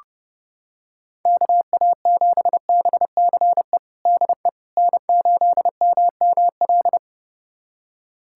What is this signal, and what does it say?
Morse code, heard as a keyed tone, about 28 wpm, 700 Hz. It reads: KA7BCE DE N8MML
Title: USA Callsign DE Callsign 30wpm